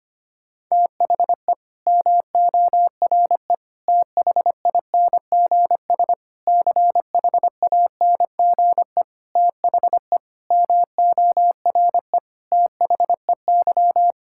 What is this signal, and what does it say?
Morse code, heard as a keyed tone, about 25 wpm, 700 Hz. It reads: THE MORE THINGS CHANGE THE MORE THEY
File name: Sets of 9 Words - 25wpm